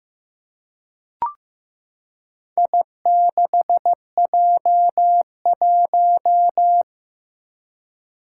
Morse code, heard as a keyed tone, about 15 wpm, 700 Hz. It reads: I6J1
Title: Four Challenging Letter-Number Combinations 15wpm